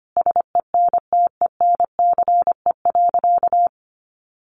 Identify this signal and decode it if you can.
Morse code, heard as a keyed tone, about 25 wpm, 700 Hz. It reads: SENTENCE.
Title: Sentences from Top 1200 Words 25wpm